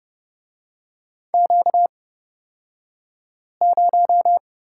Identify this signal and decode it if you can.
Morse code, heard as a keyed tone, about 30 wpm, spaced as for 5 wpm, 700 Hz. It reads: Q0
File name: Lesson 38d - Letter q - Two Characters 30wpm